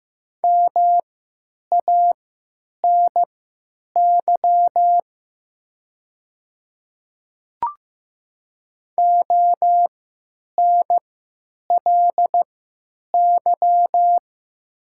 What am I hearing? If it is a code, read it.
Morse code, heard as a keyed tone, about 15 wpm, spaced as for 9 wpm, 700 Hz. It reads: MANY ONLY
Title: 3x Character Spacing - Top 200 Words - 15wpm